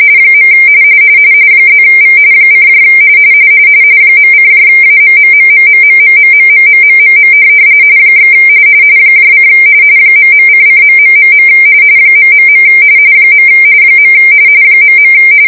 RTTY, about 45.45 baud. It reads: . +V : .:AMMMMMMMMMMMMMM / : ..::.... ... . ..